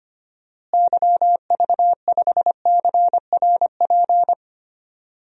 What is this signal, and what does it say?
Morse code, heard as a keyed tone, about 25 wpm, 700 Hz. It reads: YV5CRP
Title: DXCC-100 Call Signs - 6 Characters - Repeated 3x - 25wpm